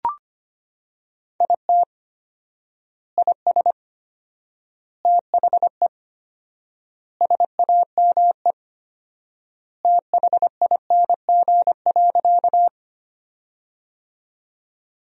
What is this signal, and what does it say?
Morse code, heard as a keyed tone, about 25 wpm, 700 Hz. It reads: IT IS THE SAME THING.
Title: 4x Word-Spacing - Sentences from Top 200 Words - 25wpm